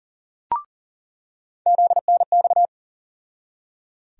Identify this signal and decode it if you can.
Morse code, heard as a keyed tone, about 38 wpm, 700 Hz. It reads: ZNX
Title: USA Call Sign Postfixes - 3 Letters 40wpm